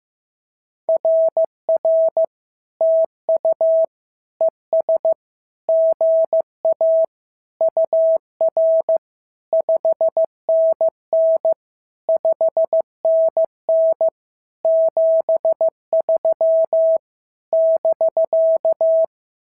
Morse code, heard as a keyed tone, about 14 wpm, 650 Hz. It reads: RR TU ES GA UR 5NN 5NN 73 <BK>
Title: SOTA QSO Practice 15wpm